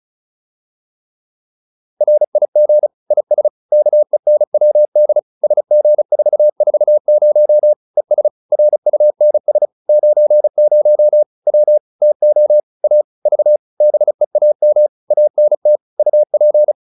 Morse code, heard as a keyed tone, about 34 wpm, 600 Hz. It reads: RIG IS KENWD SG440 ES RUNS 90 W TO A V BEAM ANT UP